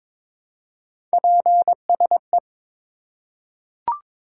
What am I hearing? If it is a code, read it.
Morse code, heard as a keyed tone, about 22 wpm, 700 Hz. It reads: PSE
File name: Top 90 QSO Elements - 4x Character Spacing - 22wpm